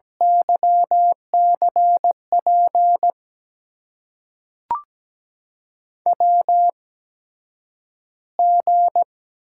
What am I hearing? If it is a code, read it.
Morse code, heard as a keyed tone, about 17 wpm, spaced as for 9 wpm, 700 Hz. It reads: YCP WG
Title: USA Call Signs - 8x Character Spacing - 17wpm